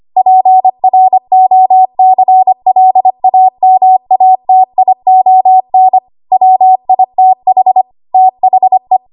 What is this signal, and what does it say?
Morse code, heard as a keyed tone, about 25 wpm, 750 Hz. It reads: PROCLAMATION WITH THE
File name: Morse / 25WPM